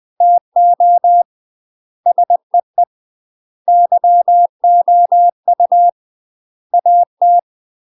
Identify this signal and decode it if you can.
Morse code, heard as a keyed tone, about 20 wpm, 700 Hz. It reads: TO SEE YOU AT
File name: UK MASTER QSOs TRAINING / 20WPM / UK MASTER QSOs 0001